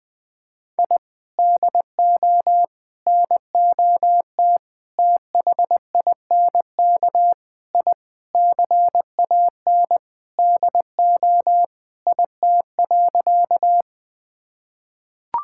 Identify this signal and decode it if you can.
Morse code, heard as a keyed tone, about 20 wpm, 700 Hz. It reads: I DO NOT THINK I CAN DO IT.
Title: Sentences with 5-Word Phrases - Spoken Beginning - 20wpm